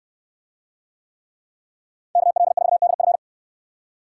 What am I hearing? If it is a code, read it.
Morse code, heard as a keyed tone, about 69 wpm, 700 Hz. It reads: KL4DF